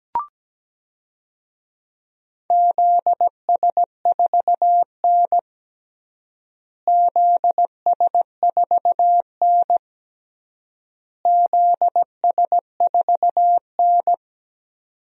Morse code, heard as a keyed tone, about 17 wpm, 700 Hz. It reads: ZS4N ZS4N ZS4N
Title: DXCC-100 Call Signs - 4 Characters - Repeated 3x - 17wpm